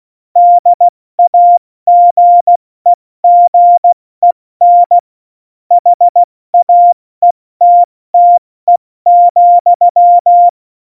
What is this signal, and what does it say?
Morse code, heard as a keyed tone, about 16 wpm, 700 Hz. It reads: DAGEGEN HAETTE,
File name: Morsekurs / Witze / Witz 21. Januar 2020, 70 F80